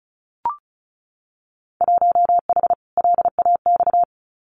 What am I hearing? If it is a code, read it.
Morse code, heard as a keyed tone, about 36 wpm, 700 Hz. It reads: 1H LAX
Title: Winter Field Day Exchanges 35wpm